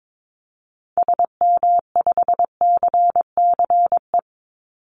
Morse code, heard as a keyed tone, about 22 wpm, 700 Hz. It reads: SM5CCE
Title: Call Signs Repeated 3x - 22wpm